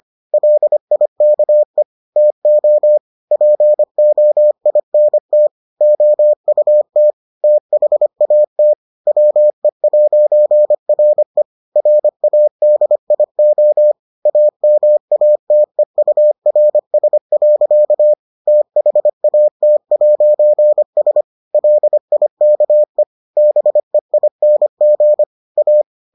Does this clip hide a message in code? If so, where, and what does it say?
Yes: Morse code from 0.3 to 25.8 s, heard as a keyed tone, about 25 wpm, 600 Hz. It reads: LIKE TO POINT OUT THAT WE'RE RADIO AMATEURS. THAT'S LIKE BEING A